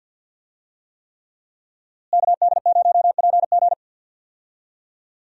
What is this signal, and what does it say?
Morse code, heard as a keyed tone, about 48 wpm, 700 Hz. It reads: KD0PG
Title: USA Call Signs - 5 Characters - Repeated 3x - 50wpm